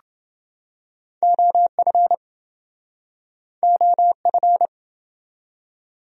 Morse code, heard as a keyed tone, about 28 wpm, 700 Hz. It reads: OF OF